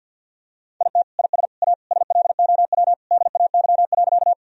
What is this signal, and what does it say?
Morse code, heard as a keyed tone, about 52 wpm, 700 Hz. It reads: IT IS A SLOW DAY.
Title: Sentences from Top 1000 Words - Review Part 2 - 50wpm